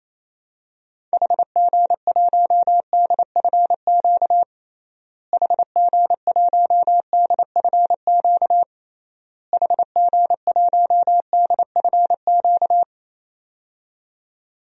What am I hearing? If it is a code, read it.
Morse code, heard as a keyed tone, about 28 wpm, 700 Hz. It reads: HG1DFQ HG1DFQ HG1DFQ
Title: DXCC-100 Call Signs - 6 Characters - Repeated 3x - 28wpm